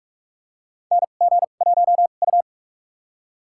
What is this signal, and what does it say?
Morse code, heard as a keyed tone, about 43 wpm, spaced as for 32 wpm, 700 Hz. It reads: NG1U
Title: USA Call Signs - 4 Characters - 2x Character Spacing - 45wpm